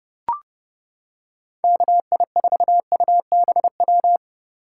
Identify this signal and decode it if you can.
Morse code, heard as a keyed tone, about 31 wpm, 700 Hz. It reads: KI4UBW